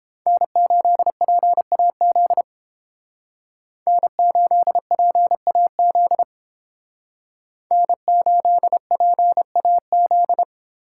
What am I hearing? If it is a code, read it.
Morse code, heard as a keyed tone, about 30 wpm, 700 Hz. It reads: N8PAZ N8PAZ N8PAZ